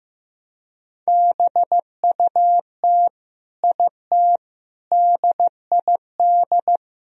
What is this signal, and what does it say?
Morse code, heard as a keyed tone, about 16 wpm, 700 Hz. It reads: BUT IT DID
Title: Sentences from Top 1000 Words - Review Part 2 - 15wpm